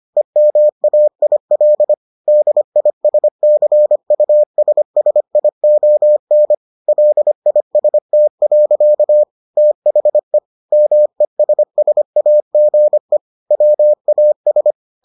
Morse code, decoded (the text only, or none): EMAIL DISCUSSION LIST. THE MESSAGE WAS